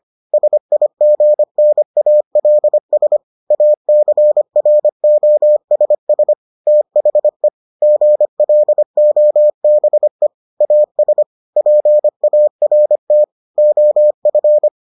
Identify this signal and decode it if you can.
Morse code, heard as a keyed tone, about 25 wpm, 600 Hz. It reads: SIGNALS ACROSS THE GLOBE AS PART OF